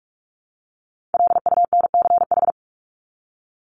Morse code, heard as a keyed tone, about 44 wpm, 700 Hz. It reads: LUNCH
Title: Top 1751-2000 Words 45wpm